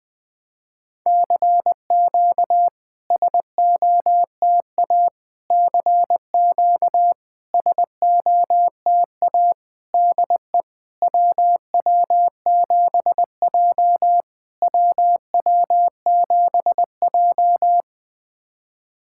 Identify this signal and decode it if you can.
Morse code, heard as a keyed tone, about 20 wpm, 700 Hz. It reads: CQ SOTA CQ SOTA DE WW7J WW7J
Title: SOTA QSO Practice 20wpm